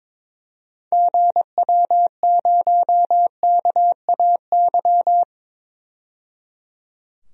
Morse code, heard as a keyed tone, about 22 wpm, 700 Hz. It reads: GW0KAY
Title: DXCC-100 Call Signs - 6 Characters - 22wpm